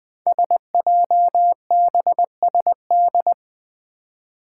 Morse code, heard as a keyed tone, about 20 wpm, 700 Hz. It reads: SJBSD
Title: Five Challenging Letter-Number Combinations 20wpm